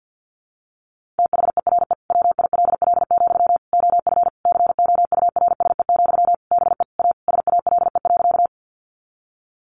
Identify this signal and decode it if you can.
Morse code, heard as a keyed tone, about 50 wpm, 700 Hz. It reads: THERE WILL, OF COURSE, BE A SALE.
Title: Sentences from Top 1400 Words 50wpm